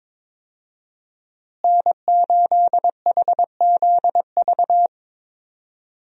Morse code, heard as a keyed tone, about 23 wpm, 700 Hz. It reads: N8HZV
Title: USA Call Signs - 5 Characters - Repeated 3x - 22wpm